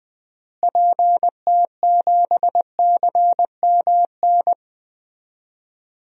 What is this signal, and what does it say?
Morse code, heard as a keyed tone, about 20 wpm, 700 Hz. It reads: PT7CMN
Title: DXCC-100 Call Signs - 6 Characters - Repeated 2x - 20wpm